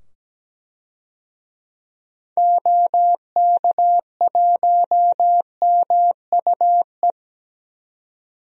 Morse code, heard as a keyed tone, about 17 wpm, 700 Hz. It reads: OK1MUE
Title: DXCC-100 Call Signs - 6 Characters - 17wpm